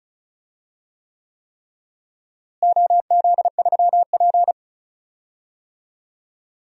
Morse code, heard as a keyed tone, about 36 wpm, 700 Hz. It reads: OZ3P
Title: DXCC-100 Call Signs - 4 Characters - Repeated 3x - 35wpm